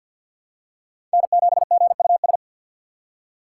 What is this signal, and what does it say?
Morse code, heard as a keyed tone, about 47 wpm, 700 Hz. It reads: N7GUS
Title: USA Call Signs - 5 Characters - 3x Character Spacing - 50wpm